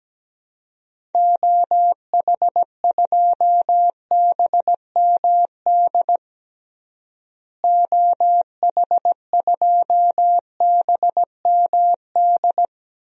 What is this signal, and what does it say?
Morse code, heard as a keyed tone, about 17 wpm, 700 Hz. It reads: OH2BMD OH2BMD
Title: DXCC-100 Call Signs - 6 Characters - Repeated 3x - 17wpm